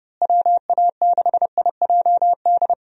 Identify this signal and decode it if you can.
Morse code, heard as a keyed tone, about 30 wpm, 700 Hz. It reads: WA6IJD
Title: USA Call Signs - 6 Characters - Repeated 3x - 30wpm